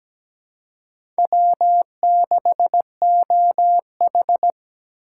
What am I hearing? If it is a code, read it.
Morse code, heard as a keyed tone, about 17 wpm, 700 Hz. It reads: W6OH
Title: USA Call Signs - 2x Character Spacing - 17wpm